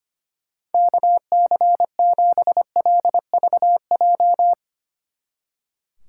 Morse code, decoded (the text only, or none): KC7LVJ